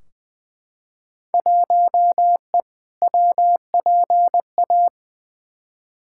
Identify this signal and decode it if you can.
Morse code, heard as a keyed tone, about 20 wpm, 700 Hz. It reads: 1E WPA
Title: Field Day Exchanges - 20wpm